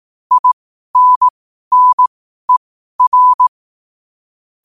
Morse code, heard as a keyed tone, about 18 wpm, spaced as for 12 wpm, 1000 Hz. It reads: INNER
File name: qotdfiles / 12wpm